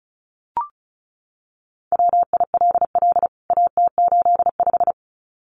Morse code, heard as a keyed tone, about 34 wpm, 700 Hz. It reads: WILL AT85
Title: CWT Exchange 35wpm